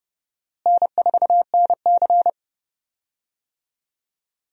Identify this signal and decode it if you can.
Morse code, heard as a keyed tone, about 30 wpm, 700 Hz. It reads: N4NC
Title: USA Call Signs - 4 Characters - Repeated 3x - 30wpm